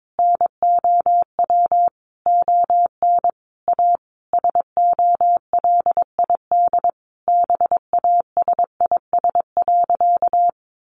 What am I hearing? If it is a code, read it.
Morse code, heard as a keyed tone, about 22 wpm, 700 Hz. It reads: NOW ON A SOLID BASIS.
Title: Sentences from Top 1500 Words 22wpm